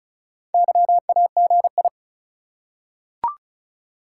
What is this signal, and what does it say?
Morse code, heard as a keyed tone, about 36 wpm, 700 Hz. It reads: YAGI
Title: Top 180 QSO Elements 35wpm